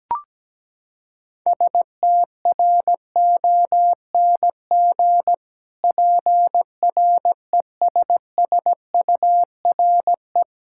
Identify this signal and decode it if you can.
Morse code, heard as a keyed tone, about 17 wpm, 700 Hz. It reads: STRONG PRESSURE
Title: Sentences from Top 1000 Words - Review Part 6 - 17wpm